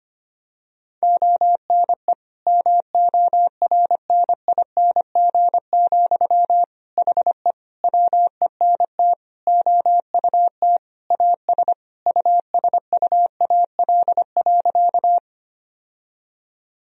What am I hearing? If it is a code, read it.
Morse code, heard as a keyed tone, about 25 wpm, 700 Hz. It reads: ONE MORNING, HE WENT OUT AS USUAL.
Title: Sentences from Top 1000 Words - Review Part 2 - 25wpm